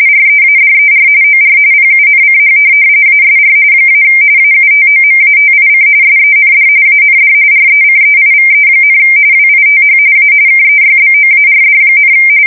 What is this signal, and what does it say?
RTTY, about 45.45 baud. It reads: READER OR PUNCH ASR - AUTOMATIC SEND AND RECEIVE, WITH A KEYBOARD,
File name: fsk170 / teletype-corp